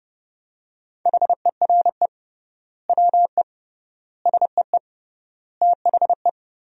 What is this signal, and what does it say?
Morse code, heard as a keyed tone, about 30 wpm, 700 Hz. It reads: HERE WE SEE THE
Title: 3x Word-Spacing - Sentences from Top 300 Words - 30wpm